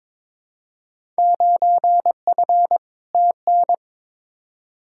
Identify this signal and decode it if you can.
Morse code, heard as a keyed tone, about 22 wpm, 700 Hz. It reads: 9F TN
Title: Field Day Exchanges Encore - 22wpm